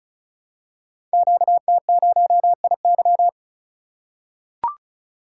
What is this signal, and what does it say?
Morse code, heard as a keyed tone, about 36 wpm, 700 Hz. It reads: QT0IY
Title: Code Groups - 5 Letter:Numbers - 35wpm